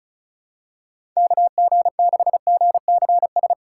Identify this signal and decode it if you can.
Morse code, heard as a keyed tone, about 36 wpm, 700 Hz. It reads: KG6GCS
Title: USA Call Signs - 6 Characters - 4x Character Spacing - 35wpm